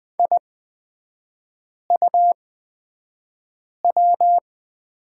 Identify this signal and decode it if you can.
Morse code, heard as a keyed tone, about 20 wpm, spaced as for 5 wpm, 690 Hz. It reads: IUW
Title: 229 Lesson 37e - Letter x - Three Characters 20wpm